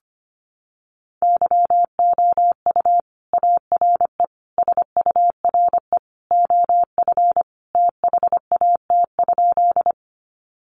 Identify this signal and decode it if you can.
Morse code, heard as a keyed tone, about 25 wpm, 700 Hz. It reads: YOU ARE SURE OF THAT?
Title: Sentences from Top 300 Words 25wpm